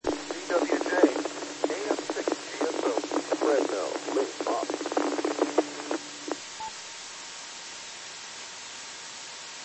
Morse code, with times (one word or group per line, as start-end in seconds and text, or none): none